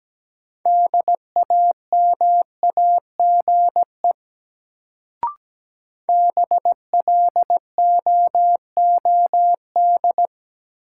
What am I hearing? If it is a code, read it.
Morse code, heard as a keyed tone, about 17 wpm, 700 Hz. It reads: DAMAGE BLOOD
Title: Sentences from Top 1100 Words - New Word Drill Alt - 17wpm